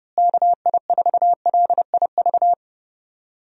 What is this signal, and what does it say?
Morse code, heard as a keyed tone, about 30 wpm, 700 Hz. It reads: KI4LIV
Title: USA Call Signs - 6 Characters - Repeated 3x - 30wpm